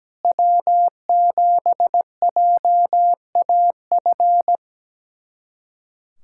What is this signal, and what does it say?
Morse code, heard as a keyed tone, about 17 wpm, 700 Hz. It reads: W7JAF
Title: USA Callsign DE Callsign - 5 Characters - 17wpm